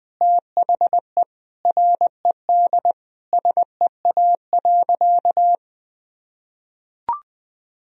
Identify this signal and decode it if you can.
Morse code, heard as a keyed tone, about 21 wpm, 700 Hz. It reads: THE RED SEA.
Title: Sentences from Top 1000 Words - Review Part 2 - 20wpm